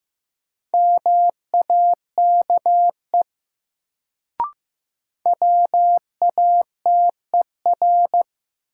Morse code, heard as a keyed tone, about 15 wpm, 700 Hz. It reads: MAKE WATER
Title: Sentences from Top 100 Words - New Word Drill - 15wpm